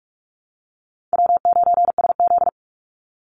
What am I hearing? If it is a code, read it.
Morse code, heard as a keyed tone, about 44 wpm, 700 Hz. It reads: W9SZ
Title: Easy Call Signs - 45wpm